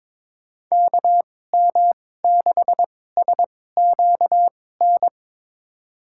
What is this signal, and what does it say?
Morse code, heard as a keyed tone, about 22 wpm, spaced as for 16 wpm, 700 Hz. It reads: KM6SQN
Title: USA Call Signs - 6 Characters - 2x Character Spacing - 22wpm